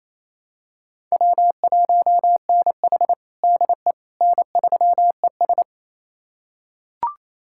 Morse code, heard as a keyed tone, about 28 wpm, 700 Hz. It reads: W1NH DE N3ES
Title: USA Callsign DE Callsign - 4 Characters - 28wpm